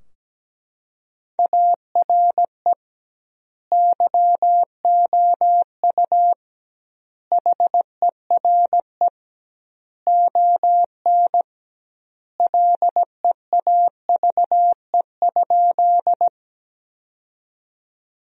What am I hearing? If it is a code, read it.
Morse code, heard as a keyed tone, about 17 wpm, 700 Hz. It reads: ARE YOU HERE ON LEAVE?
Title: 2x Word-Spacing - Sentences from Top 400 Words - 17wpm